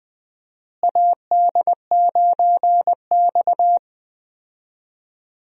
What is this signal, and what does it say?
Morse code, heard as a keyed tone, about 20 wpm, 700 Hz. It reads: AD9X
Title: USA Call Signs - 4 Characters - Repeated 2x - 20wpm